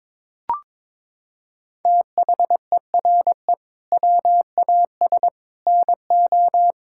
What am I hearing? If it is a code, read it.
Morse code, heard as a keyed tone, about 22 wpm, 700 Hz. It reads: THERE WAS NO
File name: practice / 22wpm / Sets of 8 Words - 22wpm